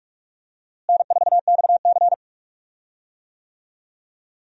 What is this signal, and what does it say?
Morse code, heard as a keyed tone, about 43 wpm, 700 Hz. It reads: N4XC
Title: USA Call Signs - 4 Characters - Repeated 3x - 45wpm